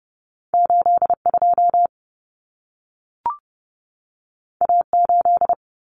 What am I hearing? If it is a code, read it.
Morse code, heard as a keyed tone, about 30 wpm, 700 Hz. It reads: 82 A8
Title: Common Cut Numbers - 2-Digit 30wpm